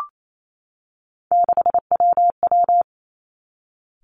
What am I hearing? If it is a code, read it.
Morse code, heard as a keyed tone, about 28 wpm, 690 Hz. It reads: 6WW
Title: Three Challenging Letter-Number Combinations 28wpm